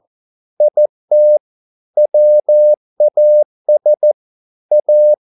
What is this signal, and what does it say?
Morse code, heard as a keyed tone, about 14 wpm, 600 Hz. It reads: IT WAS A